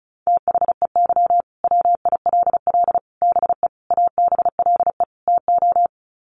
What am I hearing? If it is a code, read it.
Morse code, heard as a keyed tone, about 34 wpm, 700 Hz. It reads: THEY WILL BE ABLE TO
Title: Sets of 5 Words Encore 35wpm